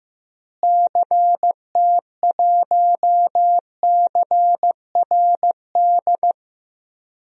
Morse code, heard as a keyed tone, about 15 wpm, 700 Hz. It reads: CT1CRD
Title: DXCC-100 Call Signs - 6 Characters - Repeated 2x - 15wpm